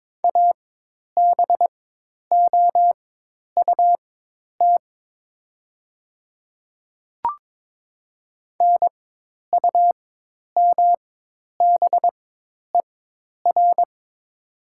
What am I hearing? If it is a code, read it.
Morse code, heard as a keyed tone, about 22 wpm, spaced as for 10 wpm, 700 Hz. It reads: ABOUT NUMBER
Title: 4x Character Spacing - Top 100 Words - 22wpm